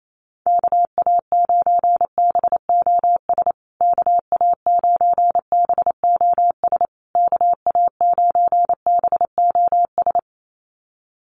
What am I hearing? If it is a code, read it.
Morse code, heard as a keyed tone, about 28 wpm, 690 Hz. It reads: KA9BOS KA9BOS KA9BOS
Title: USA Call Signs Repeated 3x 28wpm